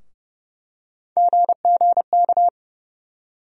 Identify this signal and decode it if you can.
Morse code, heard as a keyed tone, about 30 wpm, 700 Hz. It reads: GGK
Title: USA Call Sign Postfixes - 3 Letters 30wpm